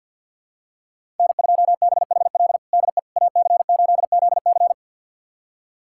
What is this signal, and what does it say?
Morse code, heard as a keyed tone, about 51 wpm, 700 Hz. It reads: N2BHL DE AC8ZC
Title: USA Callsign DE Callsign - 5 Characters - 50wpm